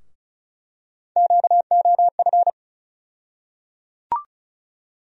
Morse code, heard as a keyed tone, about 36 wpm, 700 Hz. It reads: QOF
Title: USA Call Sign Postfixes - 3 Letters 35wpm